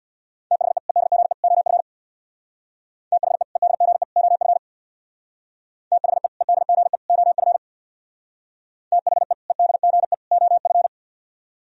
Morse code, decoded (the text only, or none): THE EDGE OF THE EDGE OF THE EDGE OF THE EDGE OF